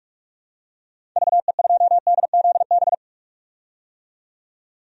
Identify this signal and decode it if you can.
Morse code, heard as a keyed tone, about 46 wpm, 700 Hz. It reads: VE2DZB
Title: DXCC-100 Call Signs - 6 Characters - 45wpm